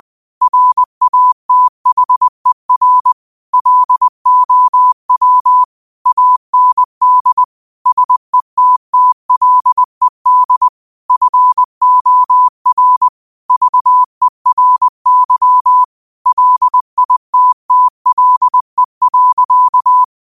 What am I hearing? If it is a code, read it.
Morse code, heard as a keyed tone, about 20 wpm, 1000 Hz. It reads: RATHER LOW AND SETTLED FOR VERY LITTLE.